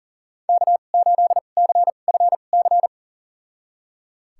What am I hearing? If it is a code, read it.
Morse code, heard as a keyed tone, about 37 wpm, spaced as for 28 wpm, 700 Hz. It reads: K8CFC
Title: USA Call Signs - 5 Characters - 2x Character Spacing - 40wpm